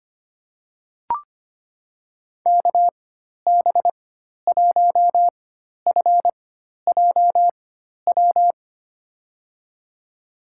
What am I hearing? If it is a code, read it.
Morse code, heard as a keyed tone, about 25 wpm, spaced as for 12 wpm, 700 Hz. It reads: KB1FJW
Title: USA Call Signs - 6 Characters - 4x Character Spacing - 25wpm